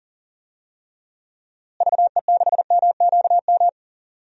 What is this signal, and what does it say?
Morse code, heard as a keyed tone, about 38 wpm, 700 Hz. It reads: VE6MQM
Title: DXCC-100 Call Signs - 6 Characters - Repeated 2x - 40wpm